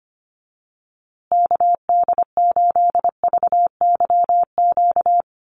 Morse code, heard as a keyed tone, about 25 wpm, 700 Hz. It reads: KD8VYQ
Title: USA Call Signs Repeated 2x 25wpm